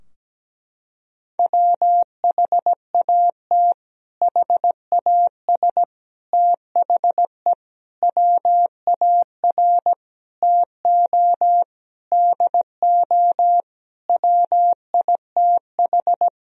Morse code, heard as a keyed tone, about 17 wpm, 700 Hz. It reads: WHAT HAS THE WAR TO DO WITH
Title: Sentences from Top 1000 Words - Review Part 2 - 17wpm